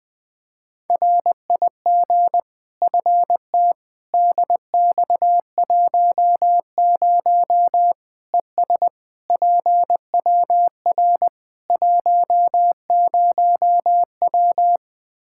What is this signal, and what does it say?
Morse code, heard as a keyed tone, about 20 wpm, 700 Hz. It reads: RIG FT DX10 ES PWR 10W
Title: Practice QSO - Exchange 2 - Info - 20wpm